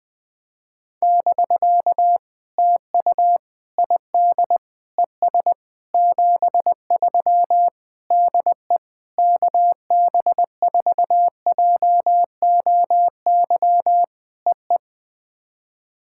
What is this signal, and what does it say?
Morse code, heard as a keyed tone, about 20 wpm, 700 Hz. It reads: <BK> TU ID ES 73 DE KB4JOY EE
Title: POTA QSO Practice - 20wpm